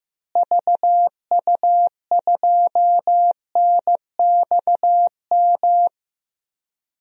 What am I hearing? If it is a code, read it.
Morse code, heard as a keyed tone, about 15 wpm, 700 Hz. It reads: VU2NXM
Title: DXCC-100 Call Signs - 6 Characters - Repeated 3x - 15wpm